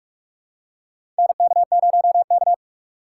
Morse code, heard as a keyed tone, about 41 wpm, 700 Hz. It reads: NK0K
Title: USA Call Signs - 2x Character Spacing - 45wpm